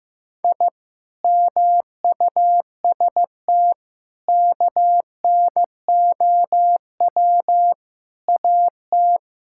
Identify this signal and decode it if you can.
Morse code, heard as a keyed tone, about 15 wpm, 700 Hz. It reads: I MUST KNOW AT
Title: Sentences from Top 1000 Words - Review Part 5 - 15wpm